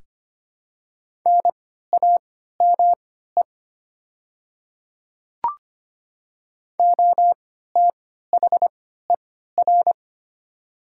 Morse code, heard as a keyed tone, about 25 wpm, spaced as for 14 wpm, 700 Hz. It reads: NAME OTHER
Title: 3x Character Spacing - Top 200 Words - 25wpm